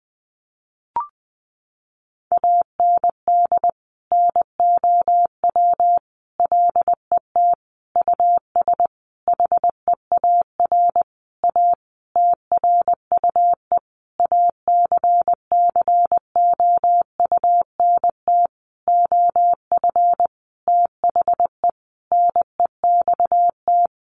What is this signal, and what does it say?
Morse code, heard as a keyed tone, about 20 wpm, 700 Hz. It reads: AND NOW LET US HEAR A TRUE ACCOUNT OF THE NEXT
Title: Sentences from Top 900 Words 20wpm